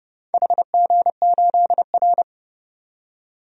Morse code, heard as a keyed tone, about 30 wpm, 700 Hz. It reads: HG8R
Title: DXCC-100 Call Signs - 4 Characters - 30wpm